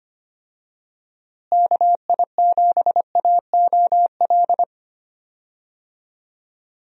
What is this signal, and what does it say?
Morse code, heard as a keyed tone, about 25 wpm, 700 Hz. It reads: KI7AOL
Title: USA Call Signs - 6 Characters - Repeated 3x - 25wpm